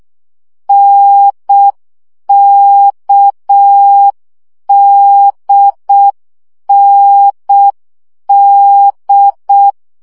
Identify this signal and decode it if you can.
Morse code, heard as a keyed tone, about 6 wpm, 800 Hz. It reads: NKDND
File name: CW gyakorlas n d k 5 baud